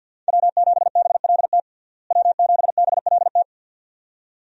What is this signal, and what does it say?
Morse code, heard as a keyed tone, about 46 wpm, 700 Hz. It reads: W7BLT W7BLT
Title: USA Call Signs - 5 Characters - Repeated 2x - 50wpm